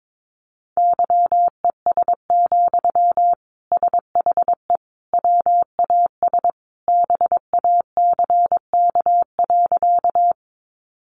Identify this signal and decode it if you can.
Morse code, heard as a keyed tone, about 22 wpm, 700 Hz. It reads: YES, SHE WAS BACK.
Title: Sentences from Top 500 Words 22wpm